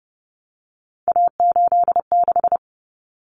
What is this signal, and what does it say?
Morse code, heard as a keyed tone, about 30 wpm, 700 Hz. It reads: A86
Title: Common Cut Numbers - 3-Digit 30wpm